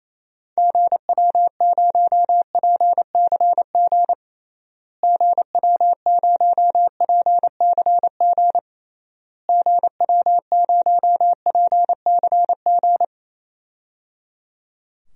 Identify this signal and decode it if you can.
Morse code, heard as a keyed tone, about 28 wpm, 700 Hz. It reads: GW0PCG GW0PCG GW0PCG
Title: DXCC-100 Call Signs - 6 Characters - Repeated 3x - 28wpm